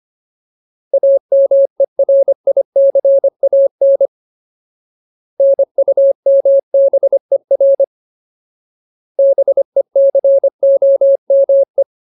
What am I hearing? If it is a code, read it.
Morse code, heard as a keyed tone, about 25 wpm, 550 Hz. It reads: AMERICAN NUMBER BECOME
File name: Ultimate Word List / 25WPM Ultimate